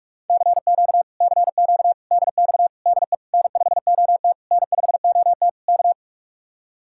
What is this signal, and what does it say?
Morse code, heard as a keyed tone, about 45 wpm, 700 Hz. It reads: CQ CQ DX DE N5OT N5OT K